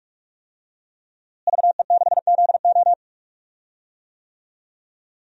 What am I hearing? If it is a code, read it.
Morse code, heard as a keyed tone, about 46 wpm, 700 Hz. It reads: VE6ZO
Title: DXCC-100 Call Signs - 5 Characters - Repeated 2x - 45wpm